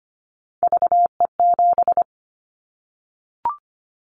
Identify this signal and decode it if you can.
Morse code, heard as a keyed tone, about 25 wpm, 700 Hz. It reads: VE7
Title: Callsign Prefixes for Top 50 DXCC - 25wpm